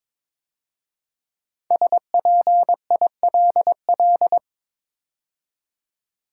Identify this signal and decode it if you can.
Morse code, heard as a keyed tone, about 22 wpm, 700 Hz. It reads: SPILL
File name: 5 Letter Words - First 3 Letters Given - 22wpm